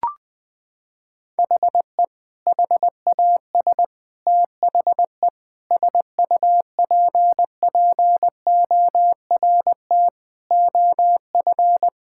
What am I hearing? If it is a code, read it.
Morse code, heard as a keyed tone, about 20 wpm, 700 Hz. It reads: HE HAS THE SUPPORT OF
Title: Sentences from Top 1000 Words - Review Part 5 - 20wpm